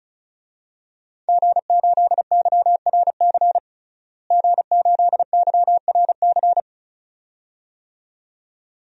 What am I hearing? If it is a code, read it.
Morse code, heard as a keyed tone, about 34 wpm, 700 Hz. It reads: G8YRC G8YRC
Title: DXCC-100 Call Signs - 5 Characters - Repeated 2x - 35wpm